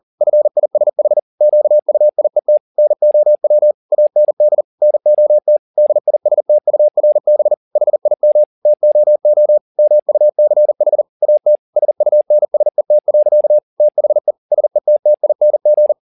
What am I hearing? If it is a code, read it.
Morse code, heard as a keyed tone, about 40 wpm, 600 Hz. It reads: FISH QUIET NOW AND NOT DISTURB HIM TOO MUCH AT SUNSET. THE SETTING
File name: the old man and the sea 2 40wpm